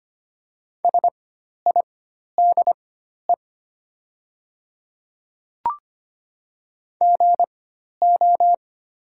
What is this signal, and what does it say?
Morse code, heard as a keyed tone, about 25 wpm, spaced as for 12 wpm, 700 Hz. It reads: SIDE GO
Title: 4x Character Spacing - Top 100 Words - 25wpm